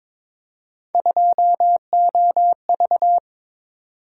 Morse code, heard as a keyed tone, about 22 wpm, 700 Hz. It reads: 2OV
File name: practice / 22wpm / Three Letter-Number Combinations - 22wpm